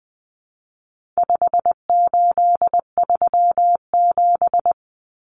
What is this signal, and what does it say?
Morse code, heard as a keyed tone, about 20 wpm, 700 Hz. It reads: 5837
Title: Numbers - 4-Digit 20wpm